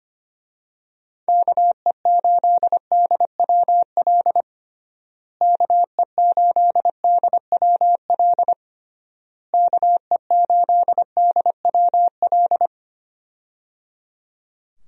Morse code, heard as a keyed tone, about 25 wpm, 700 Hz. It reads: KE8DWL KE8DWL KE8DWL
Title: USA Call Signs - 6 Characters - Repeated 3x - 25wpm